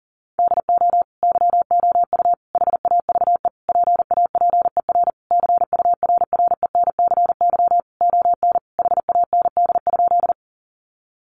Morse code, heard as a keyed tone, about 40 wpm, 700 Hz. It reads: DO YOU HAVE PAPER CURRENCY ON HAND?
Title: Sentences from Top 1500 Words 40wpm